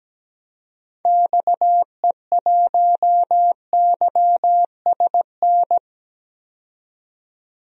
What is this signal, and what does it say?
Morse code, heard as a keyed tone, about 17 wpm, 700 Hz. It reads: XE1YSN
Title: DXCC-100 Call Signs - 6 Characters - 17wpm